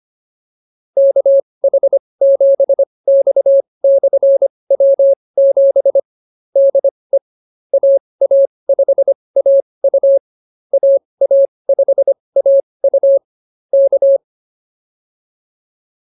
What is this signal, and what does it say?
Morse code, heard as a keyed tone, about 24 wpm, spaced as for 20 wpm, 560 Hz. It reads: KH7X/W7 DE AA5AU AA5AU K
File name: QSO104 20